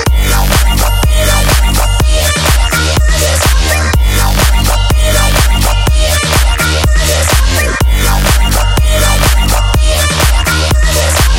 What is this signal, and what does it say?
There is no code.